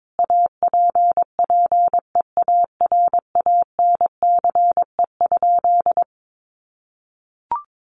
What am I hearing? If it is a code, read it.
Morse code, heard as a keyed tone, about 22 wpm, 700 Hz. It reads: APPEARANCE?
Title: Sentences from Top 1100 Words 22wpm